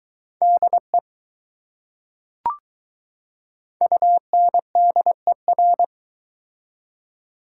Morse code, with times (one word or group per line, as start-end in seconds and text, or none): 0.4-1.0 DE
3.8-5.8 UNDER